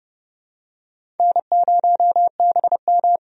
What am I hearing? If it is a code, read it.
Morse code, heard as a keyed tone, about 30 wpm, 700 Hz. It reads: N0BM
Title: USA Call Signs - 4 Characters - 30wpm